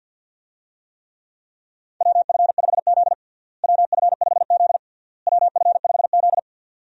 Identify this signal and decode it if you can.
Morse code, heard as a keyed tone, about 51 wpm, 700 Hz. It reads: WF5Z WF5Z WF5Z
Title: USA Call Signs - 4 Characters - Repeated 3x - 50wpm